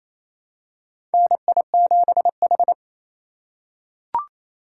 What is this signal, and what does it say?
Morse code, heard as a keyed tone, about 28 wpm, 700 Hz. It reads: NI7H
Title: USA Call Signs - 4 Characters - 4x Character Spacing - 28wpm